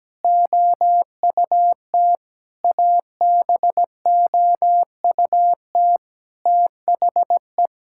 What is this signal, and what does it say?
Morse code, heard as a keyed tone, about 17 wpm, 700 Hz. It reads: OUT ABOUT THE
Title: Sentences from Top 1000 Words - Review Part 5 - 17wpm